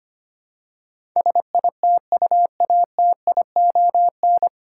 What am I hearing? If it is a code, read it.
Morse code, heard as a keyed tone, about 25 wpm, 700 Hz. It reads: SITUATION
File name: Nine Letter Words - 25wpm